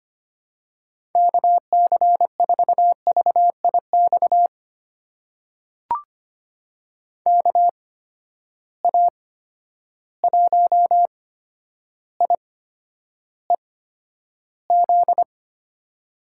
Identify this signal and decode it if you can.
Morse code, heard as a keyed tone, about 25 wpm, spaced as for 11 wpm, 700 Hz. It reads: KC4VIX KA1IEZ
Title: USA Call Signs - 6 Characters - 8x Character Spacing - 25wpm